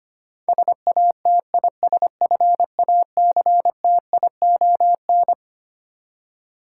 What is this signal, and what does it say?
Morse code, heard as a keyed tone, about 25 wpm, 700 Hz. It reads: SATISFACTION
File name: Twelve Letter Words - 25wpm